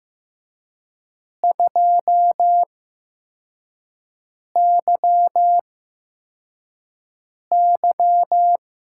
Morse code, heard as a keyed tone, about 15 wpm, spaced as for 4 wpm, 700 Hz. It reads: 2YY
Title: USA Call Signs - 6 Characters - 8x Character Spacing - 15wpm